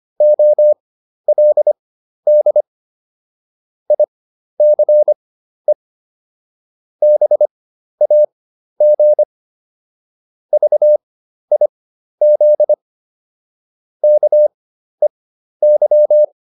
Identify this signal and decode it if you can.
Morse code, heard as a keyed tone, about 25 wpm, spaced as for 12 wpm, 600 Hz. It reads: OLD ICE BAG VIZ KEY